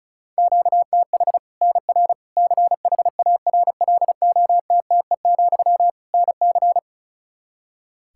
Morse code, heard as a keyed tone, about 36 wpm, 700 Hz. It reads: QTH NR CHARLOTTE, NC
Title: Practice QSO - Exchange 1 - Beginning - 35wpm